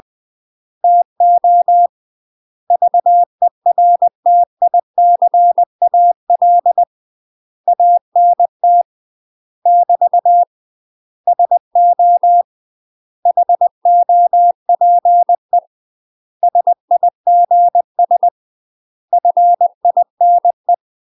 Morse code, heard as a keyed tone, about 20 wpm, 700 Hz. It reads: TO VERTICAL ANT <BT> SO HOPE SIGS FINE
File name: UK MASTER QSOs TRAINING/20WPM